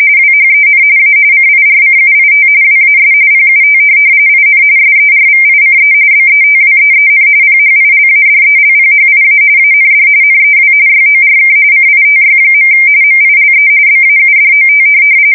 RTTY, about 45.45 baud. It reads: WHEN A CIRCUIT WAS PUT IN SERVICE BETWEEN PHILADELPHIA AND NEW YORK CITY. MILE BAUDOT
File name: fsk170 / teletype-RTTY